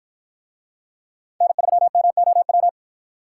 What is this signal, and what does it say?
Morse code, heard as a keyed tone, about 53 wpm, 700 Hz. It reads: N3MOW